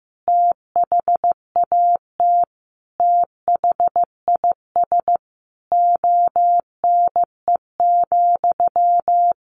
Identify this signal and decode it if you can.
Morse code, heard as a keyed tone, about 15 wpm, 700 Hz. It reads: THAT THIS ONE,
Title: Sentences from Top 200 Words 15wpm